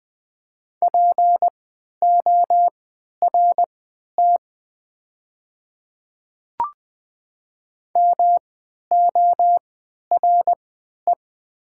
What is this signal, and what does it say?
Morse code, heard as a keyed tone, about 20 wpm, spaced as for 11 wpm, 700 Hz. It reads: PORT MORE
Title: 3x Character Spacing - Top 200 Words - 20wpm